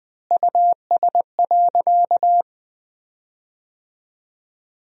Morse code, heard as a keyed tone, about 21 wpm, 700 Hz. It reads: US.
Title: 3x Word-Spacing - Sentences from Top 500 Words - 20wpm